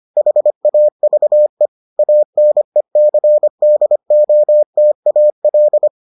Morse code, heard as a keyed tone, about 25 wpm, 600 Hz. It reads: HAVE ANECDOTAL